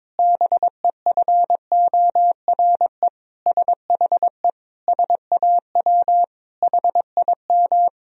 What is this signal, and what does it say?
Morse code, heard as a keyed tone, about 22 wpm, 700 Hz. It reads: BEFORE SHE SAW HIM
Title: Sentences from Top 1000 Words - Review Part 1 - 22wpm